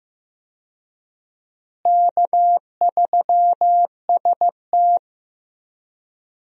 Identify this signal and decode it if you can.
Morse code, heard as a keyed tone, about 15 wpm, 700 Hz. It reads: K3ST
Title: USA Call Signs - 4 Characters - Repeated 2x - 15wpm